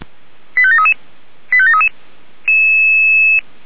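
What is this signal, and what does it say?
Five-tone selective call (ZVEI), heard as quick runs of five tones, tones of about 70 ms. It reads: 85733 85733
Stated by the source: ZVEI selective call received by SDR.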